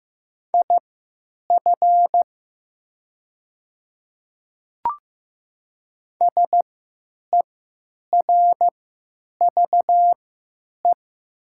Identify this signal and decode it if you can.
Morse code, heard as a keyed tone, about 15 wpm, spaced as for 9 wpm, 700 Hz. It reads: IF SERVE
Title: 3x Character Spacing - Top 500 Words - 15wpm